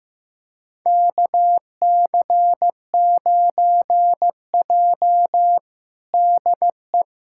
Morse code, heard as a keyed tone, about 16 wpm, 700 Hz. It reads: KC9J DE
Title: USA Callsign DE Callsign - 4 Characters - 15wpm